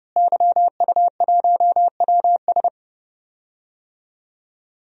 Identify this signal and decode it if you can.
Morse code, heard as a keyed tone, about 30 wpm, 700 Hz. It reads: YU1WS
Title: DXCC-100 Call Signs - 5 Characters - Repeated 2x - 30wpm